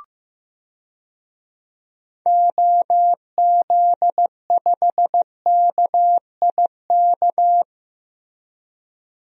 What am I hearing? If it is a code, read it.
Morse code, heard as a keyed tone, about 15 wpm, 700 Hz. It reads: OZ5KIK
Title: DXCC-100 Call Signs - 6 Characters - Repeated 2x - 15wpm